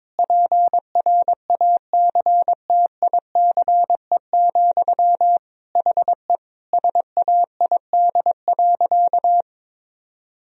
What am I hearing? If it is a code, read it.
Morse code, heard as a keyed tone, about 21 wpm, 700 Hz. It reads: PRACTICE, HE SAID.
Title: Sentences from Top 1000 Words - Review Part 4 - 22wpm